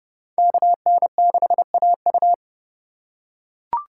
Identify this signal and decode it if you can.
Morse code, heard as a keyed tone, about 29 wpm, 700 Hz. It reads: KN6AU